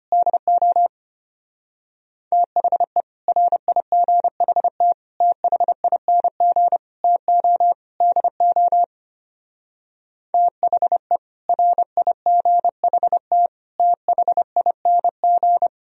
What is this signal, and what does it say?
Morse code, heard as a keyed tone, about 30 wpm, 700 Hz. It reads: DO THE RIGHT THING TO DO THE RIGHT THING
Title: Speed-Racing - Sets of 5 Words 25wpm